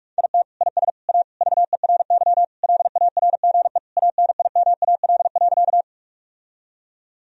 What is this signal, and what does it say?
Morse code, heard as a keyed tone, about 46 wpm, 700 Hz. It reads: IT IS A VERY LARGE ANIMAL.
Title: Sentences from Top 1000 Words - Review Part 1 - 45wpm